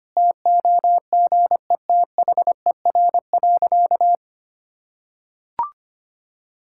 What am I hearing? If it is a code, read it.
Morse code, heard as a keyed tone, about 25 wpm, 700 Hz. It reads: TOGETHER.
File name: practice / 25wpm / Difficult Sentences with Hints - 25wpm